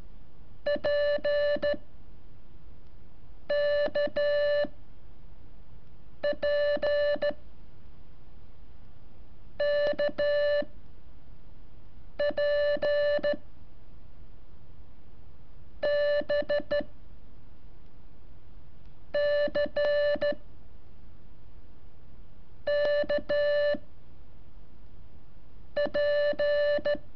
Morse code, heard as a keyed tone, about 13 wpm, 580 Hz. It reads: P K P K P B C K P
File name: curso / ejercicio 18